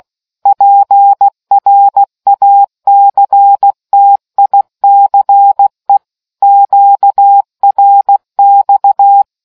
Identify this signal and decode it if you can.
Morse code, heard as a keyed tone, about 16 wpm, 800 Hz. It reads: PRACTICE QRX